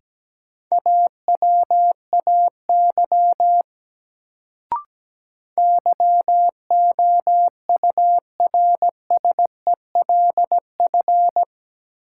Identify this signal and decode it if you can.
Morse code, heard as a keyed tone, about 17 wpm, 700 Hz. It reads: AWAY YOURSELF
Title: Sentences from Top 600 Words - New Word Drill - 17wpm